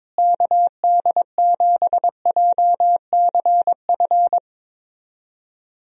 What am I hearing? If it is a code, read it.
Morse code, heard as a keyed tone, about 22 wpm, 700 Hz. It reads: KD7JCF
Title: USA Call Signs - 6 Characters - Repeated 2x - 22wpm